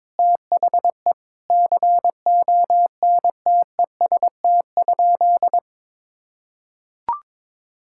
Morse code, heard as a keyed tone, about 22 wpm, 700 Hz. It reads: THE CONTEST?
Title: Sentences from Top 1600 Words 22wpm